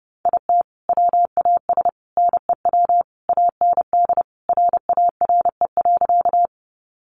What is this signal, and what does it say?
Morse code, heard as a keyed tone, about 30 wpm, 700 Hz. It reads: IT WAS NEW AND RARE.
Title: Sentences from Top 1100 Words 30wpm